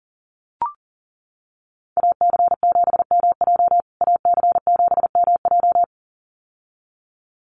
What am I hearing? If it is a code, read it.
Morse code, heard as a keyed tone, about 40 wpm, 700 Hz. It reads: AC7MJ AC7MJ
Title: USA Call Signs Repeated 2x 40wpm